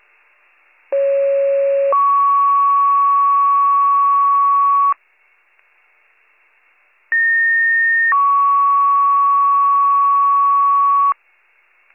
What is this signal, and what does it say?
Two-tone page, heard as a one-second tone, then a three-second tone, about 550 then 1120 Hz, about 1850 then 1120 Hz. It is heard from 0.9 to 11.1 s.